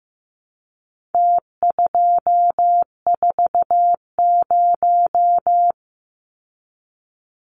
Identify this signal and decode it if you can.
Morse code, heard as a keyed tone, about 15 wpm, 700 Hz. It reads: T240
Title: Numbers - Contest Style - 4-Digit 15wpm